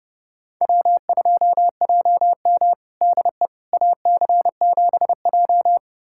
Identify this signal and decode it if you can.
Morse code, heard as a keyed tone, about 30 wpm, 700 Hz. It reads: W2JM DE AC7J
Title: USA Callsign DE Callsign 30wpm